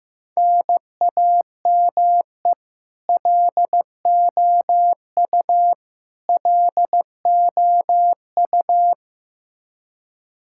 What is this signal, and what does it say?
Morse code, heard as a keyed tone, about 15 wpm, 700 Hz. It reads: NAME LOU LOU
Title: Practice QSO - Exchange 1 - Beginning - 15wpm